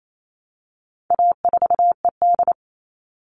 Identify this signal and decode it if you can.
Morse code, heard as a keyed tone, about 28 wpm, 700 Hz. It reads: A4ED
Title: All Cut Numbers - 4-Digit 28wpm